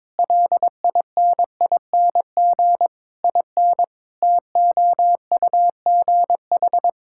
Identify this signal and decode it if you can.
Morse code, heard as a keyed tone, about 22 wpm, 700 Hz. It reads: LINING IN TOUGH
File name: Difficult Sentences with Hints - 22wpm